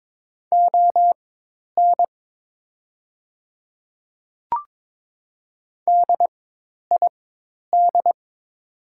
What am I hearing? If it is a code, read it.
Morse code, heard as a keyed tone, about 22 wpm, spaced as for 10 wpm, 700 Hz. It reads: ON DID
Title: 4x Character Spacing - Top 100 Words - 22wpm